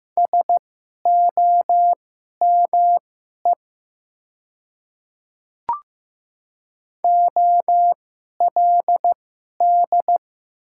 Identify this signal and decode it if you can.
Morse code, heard as a keyed tone, about 15 wpm, spaced as for 11 wpm, 700 Hz. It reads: SOME OLD
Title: 2x Character Spacing - Top 200 Words - 15wpm